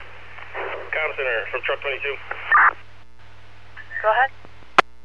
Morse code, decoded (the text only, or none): none